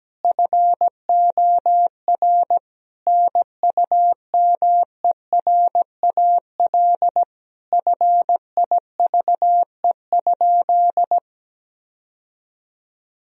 Morse code, heard as a keyed tone, about 17 wpm, 700 Hz. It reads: FOR NUMERAL FIVE?
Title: Sentences from Top 1000 Words - Review Part 2 - 17wpm